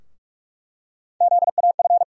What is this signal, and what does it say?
Morse code, heard as a keyed tone, about 46 wpm, 700 Hz. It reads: ZAF
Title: USA Call Sign Postfixes - 3 Letters 45wpm